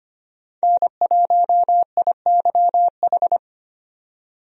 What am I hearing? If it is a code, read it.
Morse code, heard as a keyed tone, about 25 wpm, 700 Hz. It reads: N1IYH
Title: USA Call Signs - 5 Characters - 25wpm